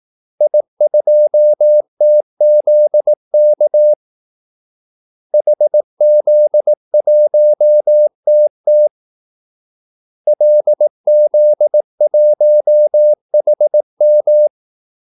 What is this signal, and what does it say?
Morse code, heard as a keyed tone, about 18 wpm, 600 Hz. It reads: I2TZK HZ1TT LZ1HM
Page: Kaldesignaler 18 wpm